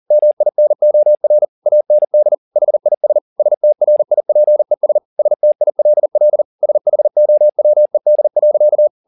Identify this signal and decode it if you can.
Morse code, heard as a keyed tone, about 40 wpm, 600 Hz. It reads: MINOR AND HIS STRIPES STILL SHOWED.
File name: the old man and the sea 2 40wpm